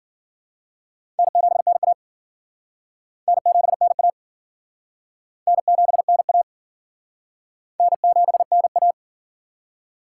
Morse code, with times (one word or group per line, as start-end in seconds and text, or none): 1.2-1.9 N7NA
3.3-4.1 N7NA
5.5-6.4 N7NA
7.8-8.9 N7NA